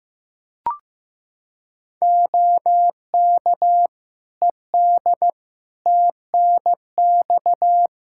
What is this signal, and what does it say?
Morse code, heard as a keyed tone, about 15 wpm, 700 Hz. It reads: OK ED TNX
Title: K1USN SST QSO Practice 15wpm